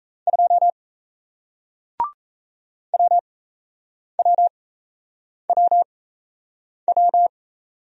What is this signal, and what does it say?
Morse code, heard as a keyed tone, about 38 wpm, spaced as for 8 wpm, 700 Hz. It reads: 2 WWWW